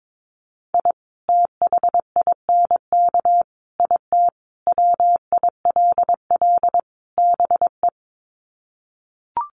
Morse code, heard as a keyed tone, about 22 wpm, 700 Hz. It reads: I THINK IT WILL BE
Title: Sets of 5 Words Encore2 22wpm